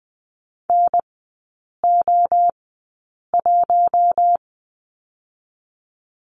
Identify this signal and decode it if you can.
Morse code, heard as a keyed tone, about 20 wpm, spaced as for 8 wpm, 700 Hz. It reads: NO1
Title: 047 Lesson 9e - Number 4 - Three Characters 20wpm